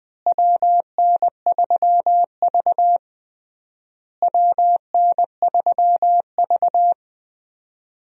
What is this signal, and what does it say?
Morse code, heard as a keyed tone, about 20 wpm, 700 Hz. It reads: WN3V WN3V
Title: USA Call Signs - 4 Characters - Repeated 3x - 20wpm